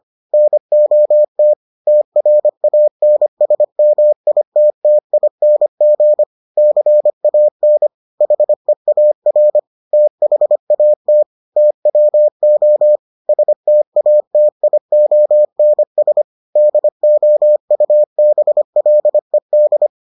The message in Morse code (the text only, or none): NOT TRANSMITTING CAN HEAR THAT TWO STATIONS DOUBLED